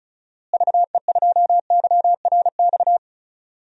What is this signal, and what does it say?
Morse code, heard as a keyed tone, about 36 wpm, 700 Hz. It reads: VE2YRX
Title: DXCC-100 Call Signs - 6 Characters - Repeated 3x - 35wpm